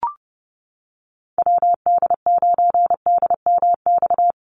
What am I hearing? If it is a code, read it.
Morse code, heard as a keyed tone, about 30 wpm, 700 Hz. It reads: WD9DMX
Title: USA Call Signs 30wpm